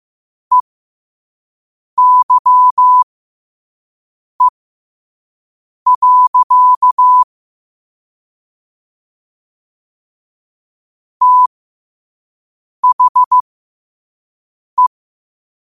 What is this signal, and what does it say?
Morse code, heard as a keyed tone, about 15 wpm, spaced as for 5 wpm, 1000 Hz. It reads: EYE. THE
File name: qotdfiles / 5wpm